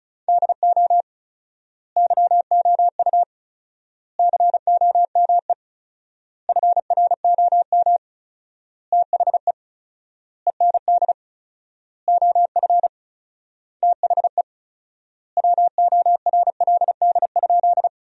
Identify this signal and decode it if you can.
Morse code, heard as a keyed tone, about 36 wpm, 700 Hz. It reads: DO YOU COME FROM THE END OF THE WORLD?
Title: 4x Word-Spacing - Sentences from Top 200 Words - 35wpm